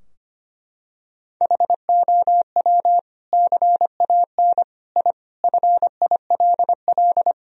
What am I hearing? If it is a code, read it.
Morse code, heard as a keyed tone, about 25 wpm, 700 Hz. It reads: HOW CAN I FILL
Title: Sentences from Top 1000 Words - Review Part 3 - 25wpm